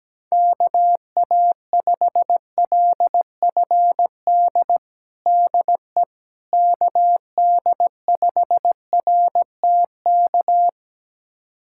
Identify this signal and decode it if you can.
Morse code, heard as a keyed tone, about 17 wpm, 700 Hz. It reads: KA5LFD DE KD5RTK
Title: USA Callsign DE Callsign - 6 Characters - 17wpm